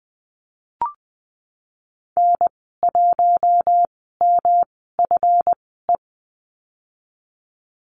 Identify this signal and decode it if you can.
Morse code, heard as a keyed tone, about 20 wpm, spaced as for 15 wpm, 700 Hz. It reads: N1MFE
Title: 140 Lesson 23g - Letter f - Call Signs 20wpm